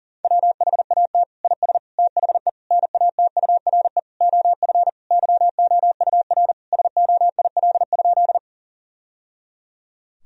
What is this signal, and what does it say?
Morse code, heard as a keyed tone, about 37 wpm, 700 Hz. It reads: WHAT IS THE NATURE OF YOUR SOIL?
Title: Sentences from Top 1000 Words - Review Part 4 - 40wpm